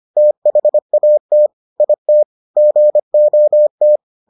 Morse code, heard as a keyed tone, about 25 wpm, 600 Hz. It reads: THAT IT GOT